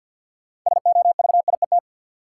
Morse code, heard as a keyed tone, about 49 wpm, 700 Hz. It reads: SOVIET
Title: Sets of 5 Words Encore - New Word Drill - 50wpm